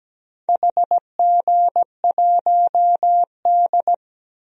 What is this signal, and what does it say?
Morse code, heard as a keyed tone, about 17 wpm, 700 Hz. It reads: HG1D
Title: DXCC-100 Call Signs - 4 Characters - 17wpm